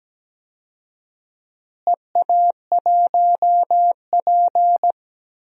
Morse code, heard as a keyed tone, about 17 wpm, 700 Hz. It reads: EA1P
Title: DXCC-100 Call Signs - 4 Characters - Repeated 3x - 17wpm